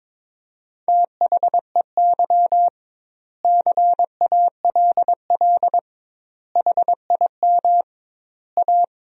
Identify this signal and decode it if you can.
Morse code, heard as a keyed tone, about 22 wpm, 700 Hz. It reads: THEY CALL HIM A
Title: 2x Word-Spacing - Sentences from Top 500 Words - 22wpm